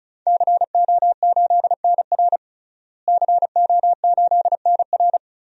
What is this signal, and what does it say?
Morse code, heard as a keyed tone, about 36 wpm, 700 Hz. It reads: CO8NR CO8NR
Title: DXCC-100 Call Signs - 5 Characters - Repeated 3x - 35wpm